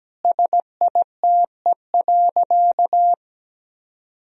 Morse code, heard as a keyed tone, about 17 wpm, 700 Hz. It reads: SITE.
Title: Sentences from Top 1000 Words - Review Part 4 - 17wpm